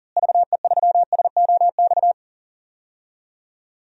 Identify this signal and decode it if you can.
Morse code, heard as a keyed tone, about 37 wpm, 700 Hz. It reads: VE3SOX
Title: DXCC-100 Call Signs - 6 Characters - Repeated 3x - 40wpm